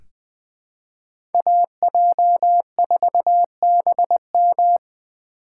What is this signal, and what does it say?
Morse code, heard as a keyed tone, about 20 wpm, 700 Hz. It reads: AJ4BM
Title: USA Call Signs - 5 Characters - 3x Character Spacing - 20wpm